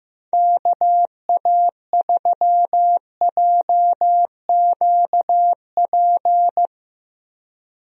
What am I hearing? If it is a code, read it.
Morse code, heard as a keyed tone, about 15 wpm, 700 Hz. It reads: KA3JQP
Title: USA Call Signs - 6 Characters - Repeated 3x - 15wpm